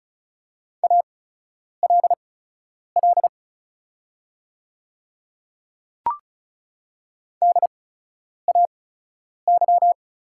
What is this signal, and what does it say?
Morse code, heard as a keyed tone, about 37 wpm, spaced as for 10 wpm, 690 Hz. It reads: ALL DAY
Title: 8x Character Spacing - Top 100 Words - 35wpm